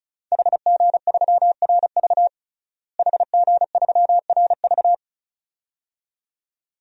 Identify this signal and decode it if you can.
Morse code, heard as a keyed tone, about 36 wpm, 700 Hz. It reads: HG3RV HG3RV
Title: DXCC-100 Call Signs - 5 Characters - Repeated 2x - 35wpm